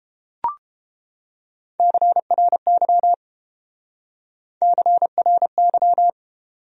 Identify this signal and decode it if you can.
Morse code, heard as a keyed tone, about 30 wpm, 700 Hz. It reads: CRY CRY